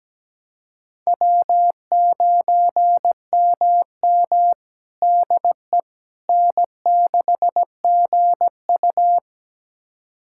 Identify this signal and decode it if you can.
Morse code, heard as a keyed tone, about 17 wpm, 700 Hz. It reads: W9MM DE N6GU
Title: USA Callsign DE Callsign - 4 Characters - 17wpm